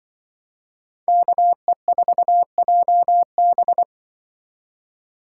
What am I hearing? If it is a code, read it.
Morse code, heard as a keyed tone, about 24 wpm, 700 Hz. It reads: KE4JB
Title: Speed Racing - USA 2x2 Call Signs 20wpm